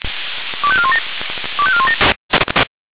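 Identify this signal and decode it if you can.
Five-tone selective call (ZVEI), heard as quick runs of five tones, tones of about 70 ms. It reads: 25417 25417
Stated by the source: Received on an SDR.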